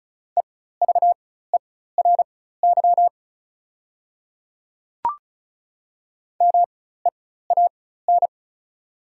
Morse code, heard as a keyed tone, about 36 wpm, spaced as for 16 wpm, 700 Hz. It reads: EVERY MEAN
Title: 4x Character Spacing - Top 200 Words - 35wpm